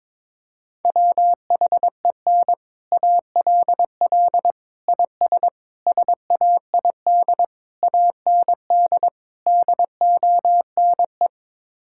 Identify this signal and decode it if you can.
Morse code, heard as a keyed tone, about 22 wpm, 700 Hz. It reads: WHEN ALL IS SAID AND DONE
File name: practice / 22wpm / Sets of 6 Words - 22wpm